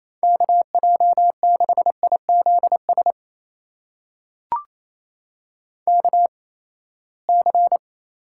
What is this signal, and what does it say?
Morse code, heard as a keyed tone, about 28 wpm, spaced as for 19 wpm, 700 Hz. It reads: KJ6IZS KC
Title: USA Call Signs - 8x Character Spacing - 28wpm